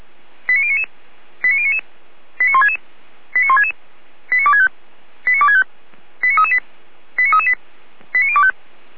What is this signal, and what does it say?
Five-tone selective call (ZVEI), heard as quick runs of five tones, tones of about 70 ms. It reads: 89099 89099 89166 89166 89265 89265 89338 89338 89025